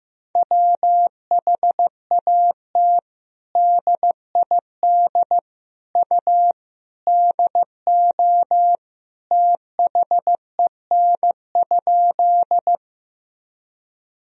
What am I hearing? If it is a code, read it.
Morse code, heard as a keyed tone, about 15 wpm, 700 Hz. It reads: WHAT DID U DO THEN?
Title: Sentences from Top 100 Words and Abbreviations 15wpm